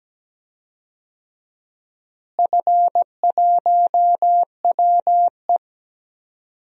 Morse code, heard as a keyed tone, about 17 wpm, 700 Hz. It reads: F1WE
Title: DXCC-100 Call Signs - 4 Characters - Repeated 2x - 17wpm